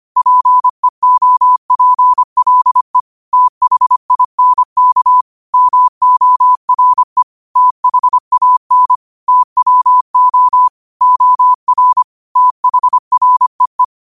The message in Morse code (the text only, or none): PEOPLE THINK MORE THAN TWO OR THREE